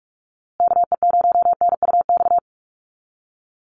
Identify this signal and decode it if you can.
Morse code, heard as a keyed tone, about 44 wpm, 690 Hz. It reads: KE0NUX
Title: USA Call Signs Repeated 3x 45wpm